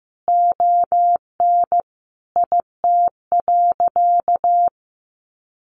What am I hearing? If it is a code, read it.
Morse code, heard as a keyed tone, about 15 wpm, 700 Hz. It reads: ON IT.
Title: Sentences from Top 1400 Words 15wpm